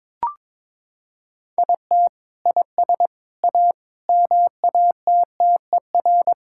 Morse code, heard as a keyed tone, about 22 wpm, 700 Hz. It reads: IT IS A MATTER
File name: practice / 22wpm / Sentences with 3-Word Phrases - 22wpm